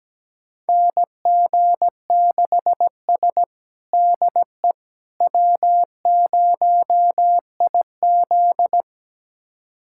Morse code, heard as a keyed tone, about 17 wpm, 700 Hz. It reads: NG6S DE W0IZ
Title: USA Callsign DE Callsign - 4 Characters - 17wpm